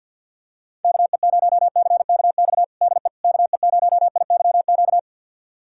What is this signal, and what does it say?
Morse code, heard as a keyed tone, about 51 wpm, 690 Hz. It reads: KE0CKX DE KE0IYQ
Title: USA Callsign DE Callsign - 6 Characters - 50wpm